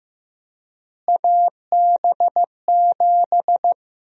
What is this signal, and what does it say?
Morse code, heard as a keyed tone, about 15 wpm, 700 Hz. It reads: AB7
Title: USA Call Sign Prefixes - Repeated 2x - 15wpm